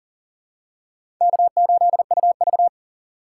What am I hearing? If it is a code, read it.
Morse code, heard as a keyed tone, about 40 wpm, 700 Hz. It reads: K8UV
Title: USA Call Signs - 4 Characters - 2x Character Spacing - 40wpm